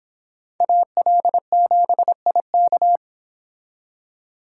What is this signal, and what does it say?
Morse code, heard as a keyed tone, about 26 wpm, 700 Hz. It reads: AL7IK